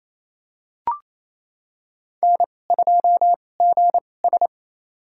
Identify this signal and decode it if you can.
Morse code, heard as a keyed tone, about 28 wpm, spaced as for 20 wpm, 700 Hz. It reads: N2GS
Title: USA Call Signs - 4 Characters - 2x Character Spacing - 28wpm